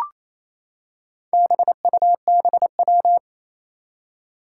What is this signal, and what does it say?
Morse code, heard as a keyed tone, about 28 wpm, 700 Hz. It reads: BUBW
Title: Four Challenging Letter-Number Combinations 28wpm